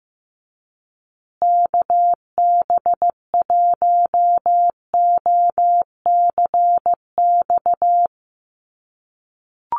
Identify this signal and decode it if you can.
Morse code, heard as a keyed tone, about 15 wpm, 700 Hz. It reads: KB1OCX
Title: USA Call Signs Repeated 3x 15wpm